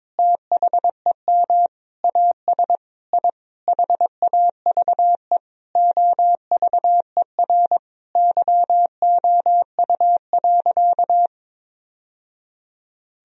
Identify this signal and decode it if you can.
Morse code, heard as a keyed tone, about 22 wpm, 700 Hz. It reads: THEM AS I HAVE OVER YOU.
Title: Sentences from Top 1000 Words - Review Part 1 - 22wpm